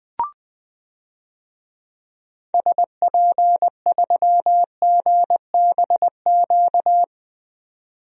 Morse code, heard as a keyed tone, about 21 wpm, 700 Hz. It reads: SP3GBQ
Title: DXCC-100 Call Signs - 6 Characters - 20wpm